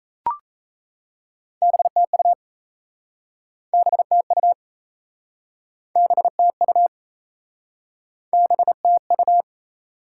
Morse code, heard as a keyed tone, about 38 wpm, 700 Hz. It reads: BTU BTU BTU BTU